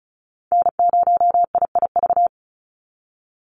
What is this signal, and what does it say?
Morse code, heard as a keyed tone, about 34 wpm, 700 Hz. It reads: N0IIV
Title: USA Call Signs 35wpm